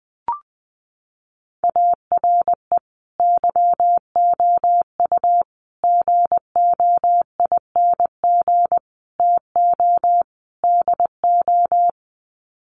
Morse code, heard as a keyed tone, about 20 wpm, 700 Hz. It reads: ARE YOU GOING TO DO
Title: Sets of 5 Words 20wpm